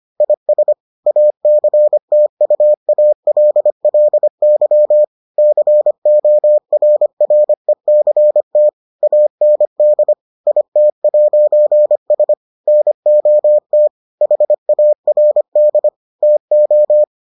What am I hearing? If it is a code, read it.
Morse code, heard as a keyed tone, about 25 wpm, 600 Hz. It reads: IS ACTUALLY CORRECT AND IT'S NOT HARD TO